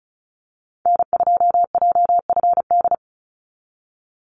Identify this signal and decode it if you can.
Morse code, heard as a keyed tone, about 34 wpm, 700 Hz. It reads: N2JFD
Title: Call Signs Repeated 2x - 35wpm